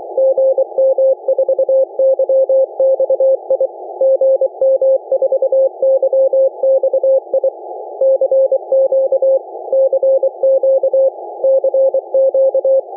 Morse code, heard as a keyed tone, about 23 wpm, 540 Hz. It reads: GM4YXI GM4YXI CQ CQ CQ